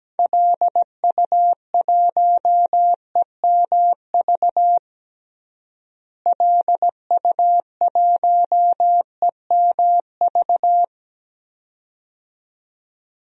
Morse code, heard as a keyed tone, about 17 wpm, 700 Hz. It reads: LU1EMV LU1EMV
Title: DXCC-100 Call Signs - 6 Characters - Repeated 2x - 17wpm